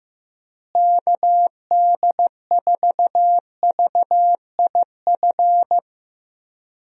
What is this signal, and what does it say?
Morse code, heard as a keyed tone, about 15 wpm, 700 Hz. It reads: KD4VIF
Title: USA Call Signs - 6 Characters - Repeated 3x - 15wpm